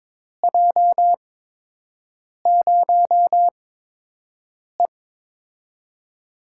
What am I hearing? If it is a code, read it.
Morse code, heard as a keyed tone, about 22 wpm, spaced as for 6 wpm, 700 Hz. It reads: J0E
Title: USA Call Signs - 8x Character Spacing - 22wpm